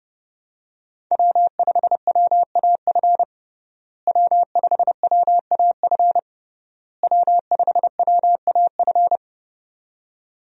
Morse code, heard as a keyed tone, about 32 wpm, 700 Hz. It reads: W5WAF W5WAF W5WAF
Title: USA Call Signs - 5 Characters - Repeated 3x - 30wpm